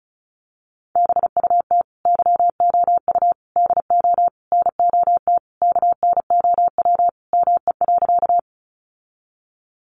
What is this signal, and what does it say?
Morse code, heard as a keyed tone, about 34 wpm, 700 Hz. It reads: BUT YOU DO NOT KNOW ME.
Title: Sentences from Top 600 Words 35wpm